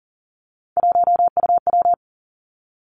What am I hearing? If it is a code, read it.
Morse code, heard as a keyed tone, about 40 wpm, 700 Hz. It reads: 1UW
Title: Three Challenging Letter-Number Combinations 40wpm